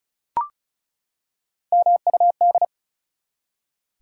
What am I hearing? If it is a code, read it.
Morse code, heard as a keyed tone, about 36 wpm, 700 Hz. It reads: MUD
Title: Three Letter Words - 35wpm